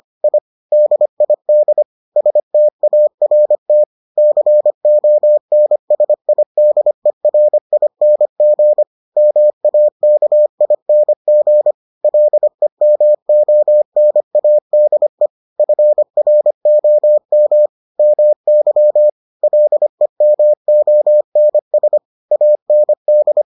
Morse code, heard as a keyed tone, about 25 wpm, 600 Hz. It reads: I DID START CONSIDERING MAKING LEMONADE FROM MY LEMONS AND